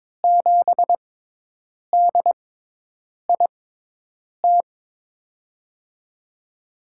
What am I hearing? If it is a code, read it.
Morse code, heard as a keyed tone, about 22 wpm, spaced as for 8 wpm, 700 Hz. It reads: 7DIT
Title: USA Call Signs - 5 Characters - 6x Character Spacing - 22wpm